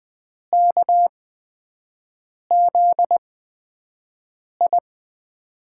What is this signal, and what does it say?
Morse code, heard as a keyed tone, about 20 wpm, spaced as for 5 wpm, 700 Hz. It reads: KZI
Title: USA Call Signs - 6 Characters - 8x Character Spacing - 20wpm